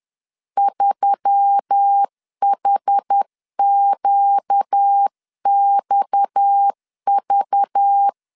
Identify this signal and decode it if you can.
Morse code, heard as a keyed tone, about 10 wpm, 800 Hz. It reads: 3HQXV
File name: CW / DET / 1 6 1500C S C 10wpm 4252